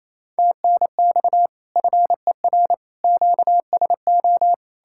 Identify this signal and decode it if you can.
Morse code, heard as a keyed tone, about 36 wpm, 700 Hz. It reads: TNX FER QSO